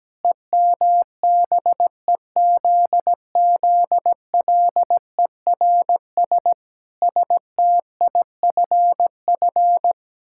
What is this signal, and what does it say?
Morse code, heard as a keyed tone, about 17 wpm, 700 Hz. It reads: EMBEZZLERS STIFF
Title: Pangram Challenge v2 - 17wpm